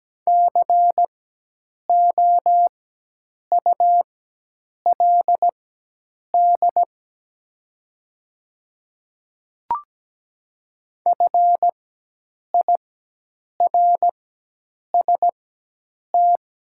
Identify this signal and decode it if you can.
Morse code, heard as a keyed tone, about 17 wpm, spaced as for 8 wpm, 700 Hz. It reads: COULD FIRST
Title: 4x Character Spacing - Top 200 Words - 17wpm